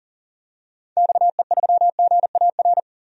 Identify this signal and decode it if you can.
Morse code, heard as a keyed tone, about 40 wpm, 700 Hz. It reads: XE3GAR
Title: DXCC-100 Call Signs - 6 Characters - 40wpm